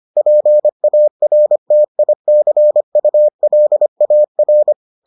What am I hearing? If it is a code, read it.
Morse code, heard as a keyed tone, about 25 wpm, 600 Hz. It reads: PARTICULAR